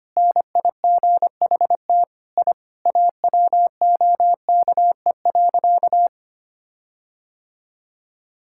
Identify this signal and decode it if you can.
Morse code, heard as a keyed tone, about 25 wpm, 700 Hz. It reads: NIGHT I AWOKE.
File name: practice / 25wpm / Sentences with 4-Word Phrases - 25wpm